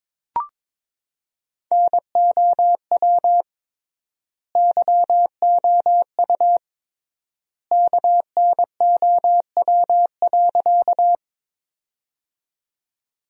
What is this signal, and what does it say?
Morse code, heard as a keyed tone, about 21 wpm, 700 Hz. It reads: NOW YOU KNOW.
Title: 3x Word-Spacing - Sentences from Top 100 Words - 22wpm